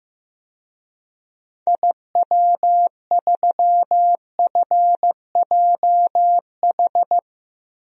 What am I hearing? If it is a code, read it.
Morse code, heard as a keyed tone, about 15 wpm, 700 Hz. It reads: IW3FJH
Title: DXCC-100 Call Signs - 6 Characters - Repeated 3x - 15wpm